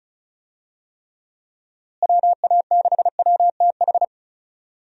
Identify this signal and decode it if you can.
Morse code, heard as a keyed tone, about 34 wpm, 700 Hz. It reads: WA6WTH
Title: USA Call Signs - 6 Characters - Repeated 3x - 35wpm